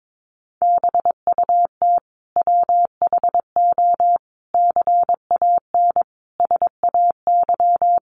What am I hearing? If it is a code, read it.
Morse code, heard as a keyed tone, about 22 wpm, 700 Hz. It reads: BUT WHO CAN SAY
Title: Sentences from Top 1500 Words 22wpm